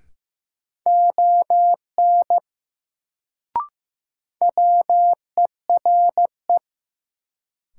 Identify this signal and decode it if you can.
Morse code, heard as a keyed tone, about 15 wpm, 700 Hz. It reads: ON WERE
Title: Sentences from Top 100 Words - New Word Drill - 15wpm